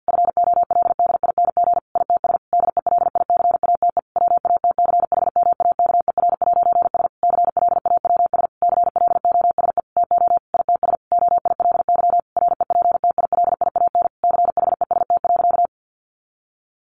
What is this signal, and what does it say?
Morse code, heard as a keyed tone, about 50 wpm, 700 Hz. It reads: FOLDING ITS DELICATE WATCHMAKER'S CLAWS CLOSE TO ITS OILY REPTILIAN CHEST.
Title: A Sound Of Thunder 50wpm